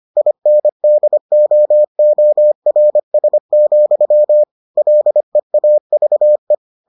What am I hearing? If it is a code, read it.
Morse code, heard as a keyed tone, about 25 wpm, 600 Hz. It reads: INDOORS, LEAVE